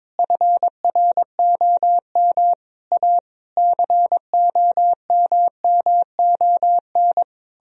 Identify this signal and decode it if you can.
Morse code, heard as a keyed tone, about 22 wpm, 700 Hz. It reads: FROM A COMMON
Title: Sentences from Top 1000 Words - Review Part 4 - 22wpm